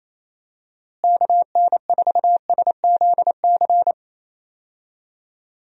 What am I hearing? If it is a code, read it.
Morse code, heard as a keyed tone, about 28 wpm, 700 Hz. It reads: KN4SZC
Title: USA Call Signs - 6 Characters - Repeated 2x - 28wpm